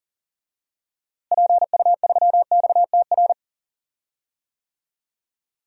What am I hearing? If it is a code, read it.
Morse code, heard as a keyed tone, about 38 wpm, 700 Hz. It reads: PU3XTR
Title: DXCC-100 Call Signs - 6 Characters - Repeated 2x - 40wpm